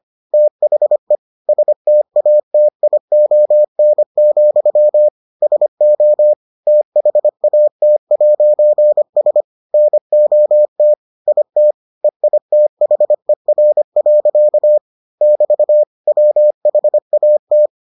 Morse code, heard as a keyed tone, about 25 wpm, 600 Hz. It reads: THE STATION, SO THAT'S NOT IT EITHER. <BT> WHAT